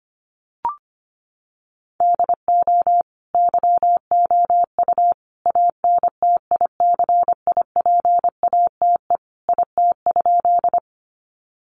Morse code, heard as a keyed tone, about 25 wpm, 700 Hz. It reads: DO YOU ANTICIPATE IT?
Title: Sentences from Top 1500 Words 25wpm